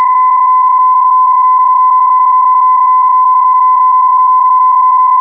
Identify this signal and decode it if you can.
PSK31, about 31.25 baud.